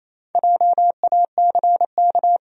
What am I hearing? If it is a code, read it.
Morse code, heard as a keyed tone, about 28 wpm, 700 Hz. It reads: JACK
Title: USA Common Names - Max 4 Letters - 28wpm